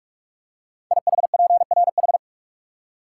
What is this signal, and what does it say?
Morse code, heard as a keyed tone, about 44 wpm, 690 Hz. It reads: IHPRH
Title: Code Groups - 5 Letters - 45wpm